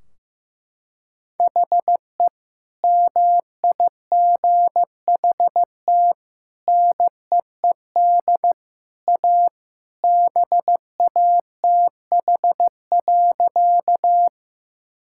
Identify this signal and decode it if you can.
Morse code, heard as a keyed tone, about 15 wpm, 700 Hz. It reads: HE MIGHT NEED A BATH.
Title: Sentences from Top 1600 Words 15wpm